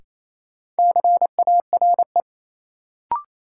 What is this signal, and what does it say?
Morse code, heard as a keyed tone, about 28 wpm, 700 Hz. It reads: CARE
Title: Sets of 3 Words - New Word Drill - 28wpm